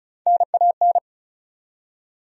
Morse code, heard as a keyed tone, about 36 wpm, 700 Hz. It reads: NAN
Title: USA Common Names - Max 4 Letters - 35wpm